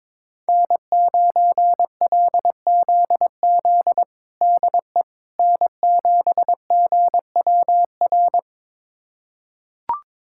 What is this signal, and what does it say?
Morse code, heard as a keyed tone, about 21 wpm, 700 Hz. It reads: N9LZZ DE N7GWR
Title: USA Callsign DE Callsign - 5 Characters - 22wpm